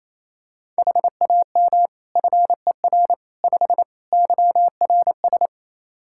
Morse code, heard as a keyed tone, about 28 wpm, 700 Hz. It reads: HAM FER 5 YRS
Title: K1USN SST QSO Practice 28wpm